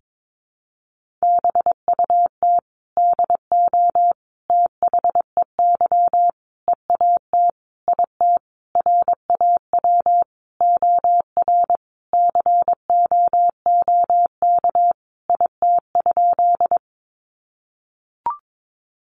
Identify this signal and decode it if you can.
Morse code, heard as a keyed tone, about 22 wpm, 700 Hz. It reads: BUT DO THEY EAT IT RAW OR COOK IT?
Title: Sentences from Top 1500 Words 22wpm